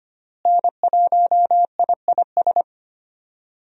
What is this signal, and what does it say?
Morse code, heard as a keyed tone, about 25 wpm, 700 Hz. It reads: N1IIS
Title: USA Call Signs - 5 Characters - Repeated 3x - 25wpm